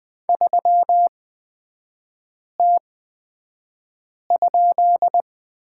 Morse code, heard as a keyed tone, about 20 wpm, spaced as for 5 wpm, 700 Hz. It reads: 3T?
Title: 250 Lesson 40e - BK - Three Characters 20wpm